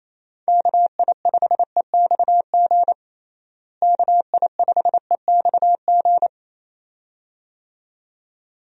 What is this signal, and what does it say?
Morse code, heard as a keyed tone, about 28 wpm, 700 Hz. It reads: KI5EXG KI5EXG
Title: USA Call Signs - 6 Characters - Repeated 2x - 28wpm